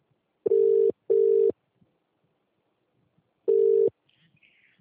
Ringback tone (UK), 400 + 450 Hz.